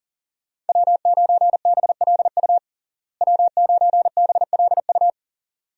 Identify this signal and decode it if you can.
Morse code, heard as a keyed tone, about 40 wpm, 700 Hz. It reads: W9BLU W9BLU
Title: USA Call Signs - 5 Characters - Repeated 2x - 40wpm